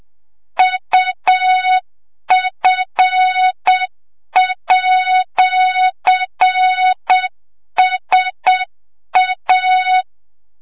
Morse code, heard as a keyed tone, about 7 wpm, 3700 Hz, with noise. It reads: UF@SA
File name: Morze-HA2MN / hang / 25 GYAKORLAT